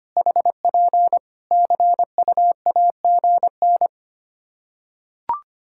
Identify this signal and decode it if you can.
Morse code, heard as a keyed tone, about 25 wpm, 700 Hz. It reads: HP CUAGN
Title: K1USN SST QSO Practice 25wpm